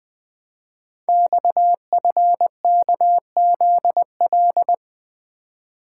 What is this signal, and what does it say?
Morse code, heard as a keyed tone, about 20 wpm, 700 Hz. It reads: XFKZL
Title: Code Groups - 5 Letters - 20wpm